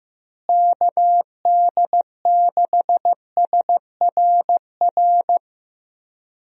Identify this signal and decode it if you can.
Morse code, heard as a keyed tone, about 15 wpm, 700 Hz. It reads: KD6SRR
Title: USA Call Signs - 6 Characters - Repeated 3x - 15wpm